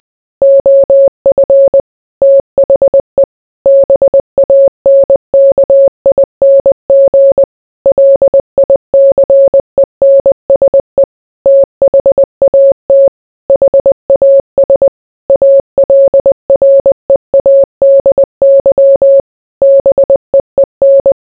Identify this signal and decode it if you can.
Morse code, heard as a keyed tone, about 20 wpm, 550 Hz. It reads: OF THE BANKING LICENSE THAT HAS ALREADY BEEN